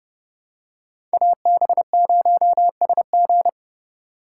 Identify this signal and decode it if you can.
Morse code, heard as a keyed tone, about 32 wpm, 700 Hz. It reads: AB0SG
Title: USA Call Signs - 5 Characters - 4x Character Spacing - 30wpm